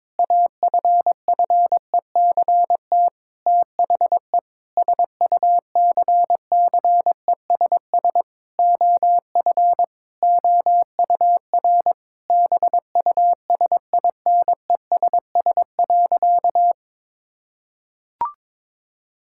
Morse code, heard as a keyed tone, about 22 wpm, 700 Hz. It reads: AFFECT THE SUCCESS OF OUR BUSINESS.
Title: Sentences from Top 1000 Words - Review Part 6 - 22wpm